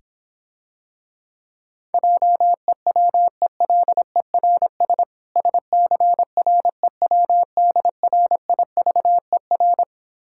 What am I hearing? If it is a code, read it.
Morse code, heard as a keyed tone, about 26 wpm, 700 Hz. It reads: JEWELERS SCREWDRIVER